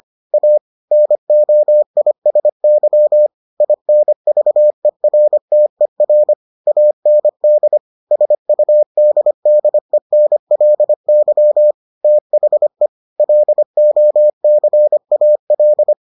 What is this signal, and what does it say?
Morse code, heard as a keyed tone, about 25 wpm, 600 Hz. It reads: A NOISY INVERTER AND SUDDENLY THE LOCAL